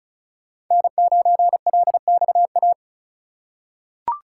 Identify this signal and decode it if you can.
Morse code, heard as a keyed tone, about 36 wpm, 700 Hz. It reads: N9LXA
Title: USA Call Signs - 5 Characters - 2x Character Spacing - 35wpm